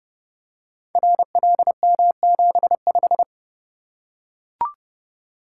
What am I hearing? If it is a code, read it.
Morse code, heard as a keyed tone, about 29 wpm, 700 Hz. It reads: RLM75
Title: Code Groups - 5 Letter:Numbers - 30wpm